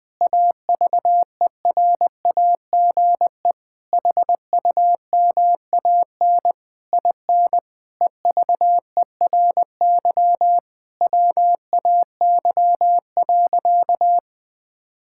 Morse code, heard as a keyed tone, about 21 wpm, 700 Hz. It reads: AVERAGE HUMAN IN EVERY WAY.
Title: Sentences from Top 1000 Words - Review Part 6 - 20wpm